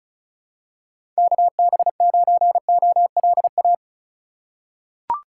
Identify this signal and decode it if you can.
Morse code, heard as a keyed tone, about 34 wpm, 700 Hz. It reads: KB9OLA
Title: USA Call Signs - 2x Character Spacing - 35wpm